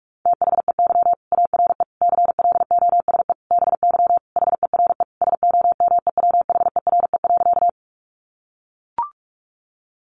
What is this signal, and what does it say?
Morse code, heard as a keyed tone, about 44 wpm, 700 Hz. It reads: THEY ARE CLOSE BY HERE SOMEWHERE.
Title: Sentences from Top 1400 Words 45wpm